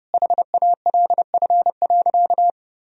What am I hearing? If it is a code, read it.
Morse code, heard as a keyed tone, about 30 wpm, 700 Hz. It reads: HALF.
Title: Sentences from Top 1000 Words - Review Part 2 - 30wpm